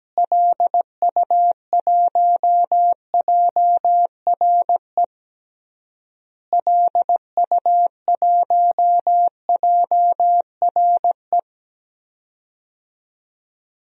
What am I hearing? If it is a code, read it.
Morse code, heard as a keyed tone, about 17 wpm, 700 Hz. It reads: LU1JRE LU1JRE
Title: DXCC-100 Call Signs - 6 Characters - Repeated 3x - 17wpm